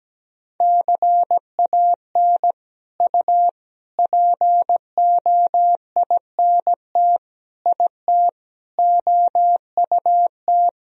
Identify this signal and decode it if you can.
Morse code, heard as a keyed tone, about 17 wpm, 700 Hz. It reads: CAN U POINT IT OUT
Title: Sentences from Top 200 Words and Abbreviations 17wpm